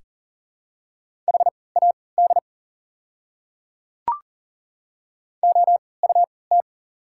Morse code, heard as a keyed tone, about 40 wpm, spaced as for 23 wpm, 690 Hz. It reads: HAD OUT
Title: 3x Character Spacing - Top 100 Words - 40wpm